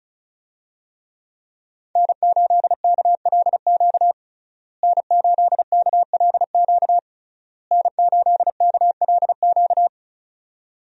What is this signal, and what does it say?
Morse code, heard as a keyed tone, about 36 wpm, 700 Hz. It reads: N8KLQ N8KLQ N8KLQ
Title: USA Call Signs - 5 Characters - Repeated 3x - 35wpm